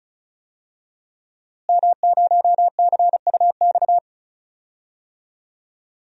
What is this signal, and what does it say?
Morse code, heard as a keyed tone, about 36 wpm, 700 Hz. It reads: M0CUX
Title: DXCC-100 Call Signs - 5 Characters - Repeated 3x - 35wpm